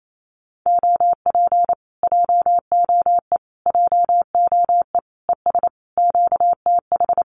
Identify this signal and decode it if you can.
Morse code, heard as a keyed tone, about 28 wpm, 700 Hz. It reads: OP JOE JOE ES QTH
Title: Common QSO Phrases Encore 28wpm